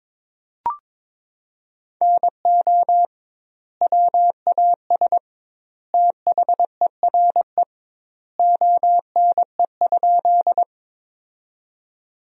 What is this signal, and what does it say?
Morse code, heard as a keyed tone, about 21 wpm, 700 Hz. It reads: NO WAS THERE ONE?
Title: 2x Word-Spacing - Sentences from Top 100 Words - 22wpm